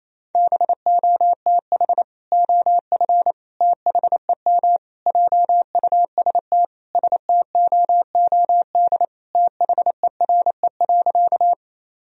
Morse code, heard as a keyed tone, about 28 wpm, 700 Hz. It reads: BOTH OF THEM JUST STOOD THERE.
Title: Sentences from Top 1000 Words - Review Part 2 - 28wpm